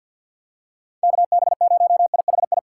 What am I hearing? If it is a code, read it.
Morse code, heard as a keyed tone, about 50 wpm, 700 Hz. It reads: KB0IHI
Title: USA Call Signs - 6 Characters - 6x Character Spacing - 50wpm